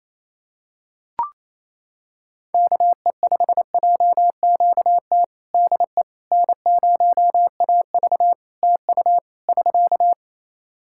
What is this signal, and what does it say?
Morse code, heard as a keyed tone, about 28 wpm, 700 Hz. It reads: KE5JQT DE N0AV TU <SK>
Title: Practice QSO - Exchange 4 - Ending - 28wpm